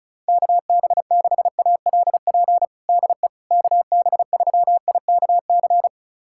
Morse code, heard as a keyed tone, about 36 wpm, 700 Hz. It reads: KB6ALP DE KB3IKC
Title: USA Callsign DE Callsign 35wpm